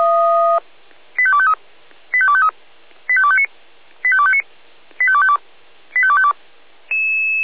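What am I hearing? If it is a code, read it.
Five-tone selective call (ZVEI), heard as quick runs of five tones, tones of about 70 ms. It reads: 85253 85253 85259 85259 85252 85252
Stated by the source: ZVEI five-tone selective call received on SDR.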